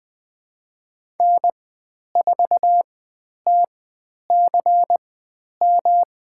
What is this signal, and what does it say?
Morse code, heard as a keyed tone, about 20 wpm, spaced as for 10 wpm, 700 Hz. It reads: N4TCM
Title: 109 Lesson 18g - Letter m - Call Signs 20wpm